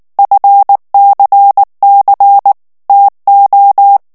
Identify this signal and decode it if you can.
Morse code, heard as a keyed tone, about 19 wpm, 800 Hz. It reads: FCC TO
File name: ANGLAIS texte clair 3 vitesse 18 mots minute